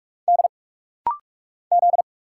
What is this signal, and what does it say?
Morse code, heard as a keyed tone, about 46 wpm, 700 Hz. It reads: D Z
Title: Single Letter-Number - Rapid-Fire 45wpm